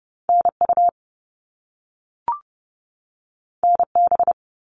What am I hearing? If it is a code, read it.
Morse code, heard as a keyed tone, about 30 wpm, 700 Hz. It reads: NU NB
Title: Canadian Provinces and Territories 30wpm